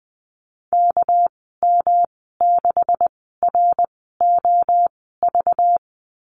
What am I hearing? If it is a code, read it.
Morse code, heard as a keyed tone, about 20 wpm, spaced as for 14 wpm, 700 Hz. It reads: KM6ROV
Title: 213 Lesson 34g - Number 8 - Call Signs 20wpm